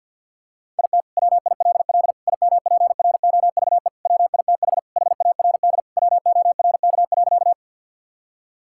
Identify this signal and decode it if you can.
Morse code, heard as a keyed tone, about 49 wpm, 700 Hz. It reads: IT WILL IMPROVE WITH HARD WORK.
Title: Sentences from Top 1000 Words - Review Part 5 - 50wpm